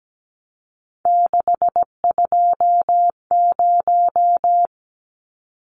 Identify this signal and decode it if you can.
Morse code, heard as a keyed tone, about 17 wpm, 700 Hz. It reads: 620
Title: Numbers - 3-Digit 17wpm